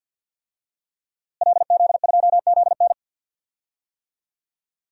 Morse code, heard as a keyed tone, about 51 wpm, 700 Hz. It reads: LZ2ZN
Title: DXCC-100 Call Signs - 5 Characters - Repeated 3x - 50wpm